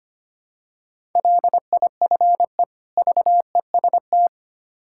Morse code, heard as a keyed tone, about 27 wpm, 700 Hz. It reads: LIFE VEST
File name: practice / 25wpm / Difficult Words - Speed-Racing with Hints - 25wpm